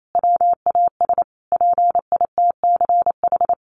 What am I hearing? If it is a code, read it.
Morse code, heard as a keyed tone, about 28 wpm, 700 Hz. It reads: WAS PITCH
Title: Sentences from Top 1300 Words 28wpm